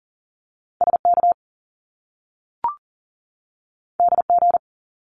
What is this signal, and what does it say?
Morse code, heard as a keyed tone, about 40 wpm, 700 Hz. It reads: SK DG
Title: Top Two Letter Combinations in Words 40wpm